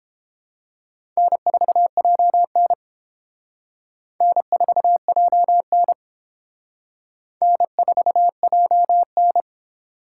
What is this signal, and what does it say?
Morse code, heard as a keyed tone, about 29 wpm, 700 Hz. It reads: N4JN N4JN N4JN